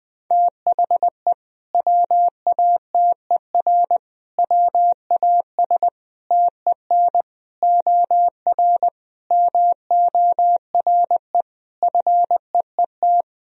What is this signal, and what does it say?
Morse code, heard as a keyed tone, about 20 wpm, 700 Hz. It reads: THE WATER WAS TEN OR MORE FEET
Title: Sentences from Top 1000 Words - Review Part 3 - 20wpm